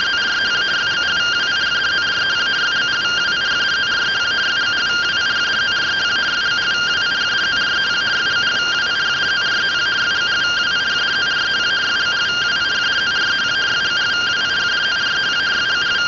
Two-tone teleprinter signal, about 170 Hz shift.